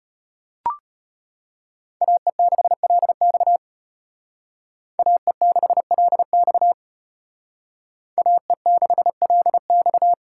Morse code, heard as a keyed tone, about 29 wpm, 700 Hz. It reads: AE6LX AE6LX AE6LX
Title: Speed Racing - USA 2x2 Call Signs 25wpm